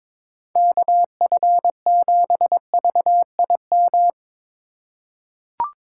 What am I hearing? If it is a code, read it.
Morse code, heard as a keyed tone, about 21 wpm, 700 Hz. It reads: KF7VIM
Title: USA Call Signs - 6 Characters - 2x Character Spacing - 22wpm